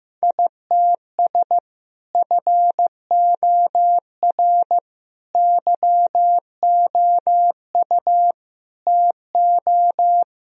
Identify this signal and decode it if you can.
Morse code, heard as a keyed tone, about 15 wpm, 700 Hz. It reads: ITS FOR YOU TO
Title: Sentences from Top 1000 Words - Review Part 3 - 15wpm